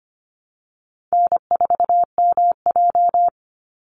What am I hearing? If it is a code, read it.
Morse code, heard as a keyed tone, about 25 wpm, 700 Hz. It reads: N4MJ
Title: NA SOTA Chaser Callsigns 25wpm